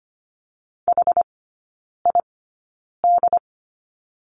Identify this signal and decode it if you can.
Morse code, heard as a keyed tone, about 25 wpm, spaced as for 9 wpm, 690 Hz. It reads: HID
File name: Lesson 14e - Number 2 - Three Characters 25wpm